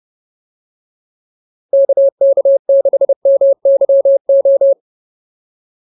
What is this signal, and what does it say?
Morse code, heard as a keyed tone, about 30 wpm, 550 Hz. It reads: KK6MYO